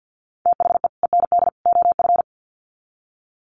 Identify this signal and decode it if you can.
Morse code, heard as a keyed tone, about 50 wpm, 700 Hz. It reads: THE END OF
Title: Sets of 3 Words 50wpm